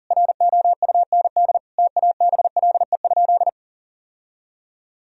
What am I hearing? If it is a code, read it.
Morse code, heard as a keyed tone, about 40 wpm, 700 Hz. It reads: ROUND TABLE?
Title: Sentences from Top 1000 Words - Review Part 2 - 40wpm